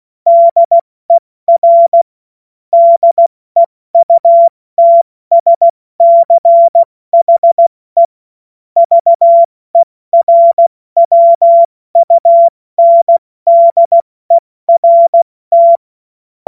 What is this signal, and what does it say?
Morse code, heard as a keyed tone, about 16 wpm, 680 Hz. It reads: DER DEUTSCHE VERWUNDERT
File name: Morsekurs / Witze / Witz 07.November 2023 70 F80